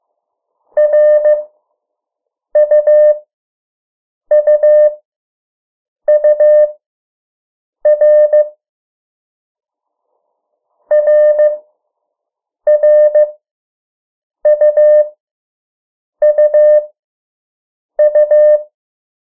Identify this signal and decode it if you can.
Morse code, heard as a keyed tone, about 15 wpm, spaced as for 6 wpm, 600 Hz. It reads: RUUUR RRUUU
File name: morse mp3 / 03 urok 2